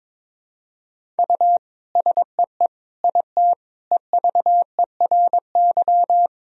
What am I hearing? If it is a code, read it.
Morse code, heard as a keyed tone, about 22 wpm, 700 Hz. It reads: U SEE IT EVERY
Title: Sentences from Top 200 Words and Abbreviations 22wpm